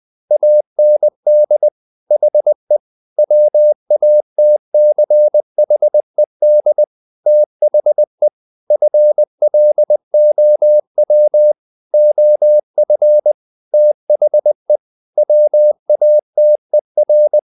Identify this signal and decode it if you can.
Morse code, heard as a keyed tone, about 20 wpm, 600 Hz. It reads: AND HE WATCHED THE FLOW OF THE WATER
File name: the old man and the sea part 2 20wpm